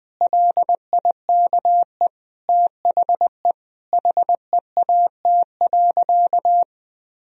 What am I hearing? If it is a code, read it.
Morse code, heard as a keyed tone, about 20 wpm, 700 Hz. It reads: LIKE THE HEAT.
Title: Sentences from Top 1000 Words - Review Part 3 - 20wpm